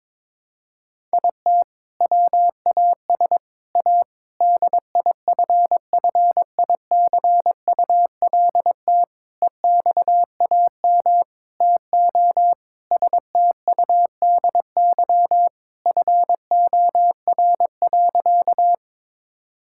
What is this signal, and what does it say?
Morse code, heard as a keyed tone, about 22 wpm, 700 Hz. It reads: IT WAS A DIFFICULT EXAM TO STUDY FOR.
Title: Sentences from Top 1000 Words - Review Part 5 - 22wpm